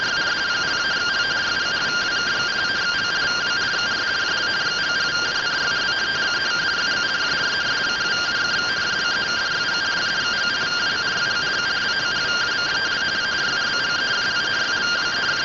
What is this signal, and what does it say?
Two-tone teleprinter signal, about 170 Hz shift.
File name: findings / MR CORONAL fr1 -131.gen-02 rtty rx